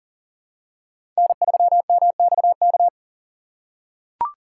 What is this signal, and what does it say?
Morse code, heard as a keyed tone, about 40 wpm, 700 Hz. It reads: N3MXK
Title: USA Call Signs - 5 Characters - 2x Character Spacing - 40wpm